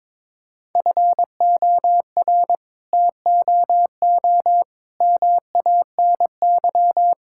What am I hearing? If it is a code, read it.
Morse code, heard as a keyed tone, about 22 wpm, 700 Hz. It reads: FOR TOO MANY
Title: Sentences from Top 1000 Words - Review Part 6 - 22wpm